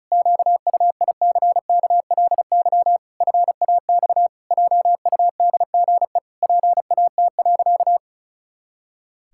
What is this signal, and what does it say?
Morse code, heard as a keyed tone, about 34 wpm, 700 Hz. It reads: QUICKLY FAX JUDGE PAT.
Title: Pangram Challenge v2 - 35wpm